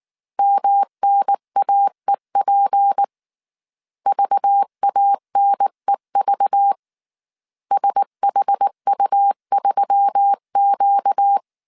Morse code, heard as a keyed tone, about 19 wpm, 800 Hz. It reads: MNAEP VANEV SHU3Q
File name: CW / DET / 1 9 1500C S B 18wpm 2551